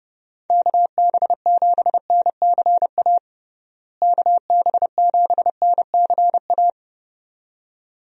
Morse code, heard as a keyed tone, about 30 wpm, 700 Hz. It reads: KB7NCA KB7NCA
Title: USA Call Signs - 6 Characters - Repeated 2x - 30wpm